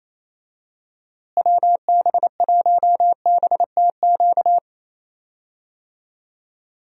Morse code, heard as a keyed tone, about 28 wpm, 700 Hz. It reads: WB1BTQ
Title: USA Call Signs - 6 Characters - Repeated 3x - 28wpm